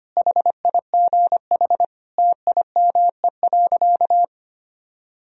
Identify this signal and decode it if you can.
Morse code, heard as a keyed tone, about 25 wpm, 700 Hz. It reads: HIGH TIME.
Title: Sentences from Top 1000 Words - Review Part 1 - 25wpm